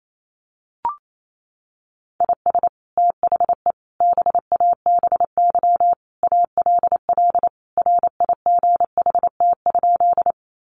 Morse code, heard as a keyed tone, about 28 wpm, 700 Hz. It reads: IS THE BABY ALL RIGHT?
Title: Sentences from Top 1300 Words 28wpm